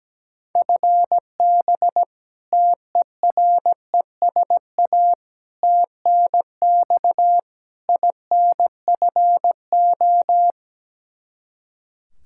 Morse code, heard as a keyed tone, about 17 wpm, 700 Hz. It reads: FB TERESA TNX INFO
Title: Practice QSO - Exchange 3 - More Info - 17wpm